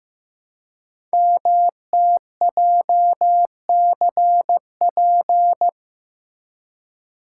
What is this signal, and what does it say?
Morse code, heard as a keyed tone, about 15 wpm, 700 Hz. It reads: MTJCP
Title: Code Groups - 5 Letters - 15wpm